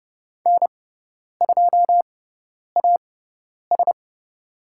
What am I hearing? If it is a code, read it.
Morse code, heard as a keyed tone, about 30 wpm, spaced as for 10 wpm, 700 Hz. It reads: N2AS
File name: Lesson 17g - Letter u - Call Signs 30wpm